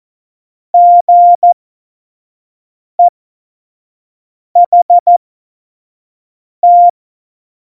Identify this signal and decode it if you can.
Morse code, heard as a keyed tone, about 14 wpm, spaced as for 5 wpm, 700 Hz. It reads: GEHT
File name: Morsekurs / Witze / Witz 27 Juni 17, Tempo 25 F70